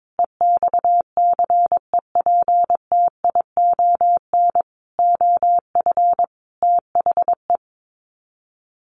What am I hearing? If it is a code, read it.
Morse code, heard as a keyed tone, about 22 wpm, 700 Hz. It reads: EXCEPTION OF THE
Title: Sets of 5 Words Encore 22wpm